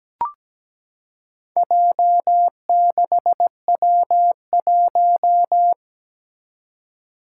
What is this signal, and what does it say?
Morse code, heard as a keyed tone, about 17 wpm, 700 Hz. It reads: J6W1
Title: Four Challenging Letter-Number Combinations 17wpm